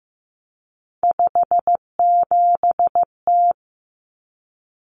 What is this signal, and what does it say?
Morse code, heard as a keyed tone, about 15 wpm, 700 Hz. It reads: 57T
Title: Common Cut Numbers - 3-Digit 15wpm